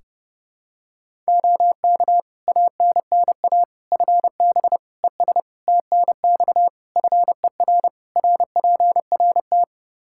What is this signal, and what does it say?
Morse code, heard as a keyed tone, about 30 wpm, 700 Hz. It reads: OK ANNA FB ES TNX FER RPRT
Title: Practice QSO - Exchange 2 - Info - 30wpm